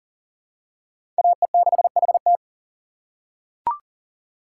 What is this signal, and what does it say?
Morse code, heard as a keyed tone, about 40 wpm, 700 Hz. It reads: AE6HT
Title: USA Call Signs - 5 Characters - 4x Character Spacing - 40wpm